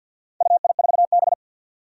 Morse code, heard as a keyed tone, about 51 wpm, 700 Hz. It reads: UI4B
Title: Four Challenging Letter-Number Combinations 50wpm